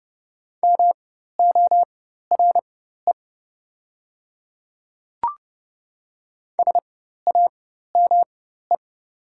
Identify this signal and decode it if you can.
Morse code, heard as a keyed tone, about 30 wpm, spaced as for 14 wpm, 700 Hz. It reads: MORE SAME
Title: 4x Character Spacing - Top 200 Words - 30wpm